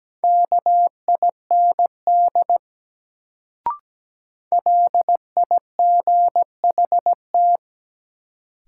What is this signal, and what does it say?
Morse code, heard as a keyed tone, about 17 wpm, 700 Hz. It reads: KIND LIGHT
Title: Sentences from Top 200 Words - New Word Drill - 17wpm